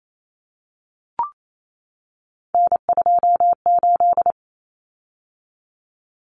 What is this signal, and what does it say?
Morse code, heard as a keyed tone, about 28 wpm, 700 Hz. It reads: N28
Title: Common Cut Numbers - 3-Digit 28wpm